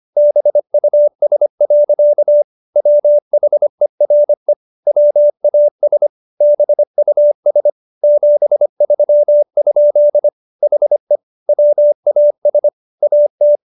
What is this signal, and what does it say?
Morse code, heard as a keyed tone, about 25 wpm, 590 Hz. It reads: BUS. WHERE WAS BUS 73? HE WAS AT